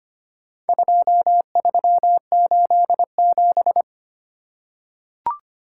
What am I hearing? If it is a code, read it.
Morse code, heard as a keyed tone, about 25 wpm, 700 Hz. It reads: 2387
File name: practice / 25wpm / Four Letter-Number Combinations - 25wpm